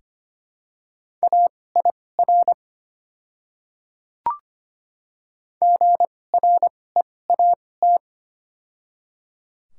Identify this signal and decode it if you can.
Morse code, heard as a keyed tone, about 25 wpm, spaced as for 18 wpm, 690 Hz. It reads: AIR GREAT
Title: 2x Character Spacing - Top 200 Words - 25wpm